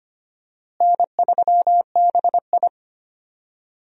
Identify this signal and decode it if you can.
Morse code, heard as a keyed tone, about 25 wpm, 700 Hz. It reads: N3BI
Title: USA Call Signs - 4 Characters - Repeated 2x - 25wpm